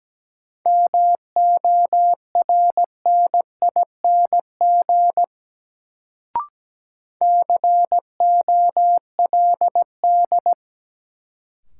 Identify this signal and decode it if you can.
Morse code, heard as a keyed tone, about 17 wpm, 700 Hz. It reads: MORNING COLD
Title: Sentences from Top 400 Words - New Word Drill - 17wpm